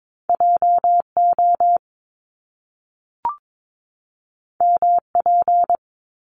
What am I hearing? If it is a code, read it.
Morse code, heard as a keyed tone, about 22 wpm, 700 Hz. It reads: JO MP
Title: Top Two Letter Combinations in Words 22wpm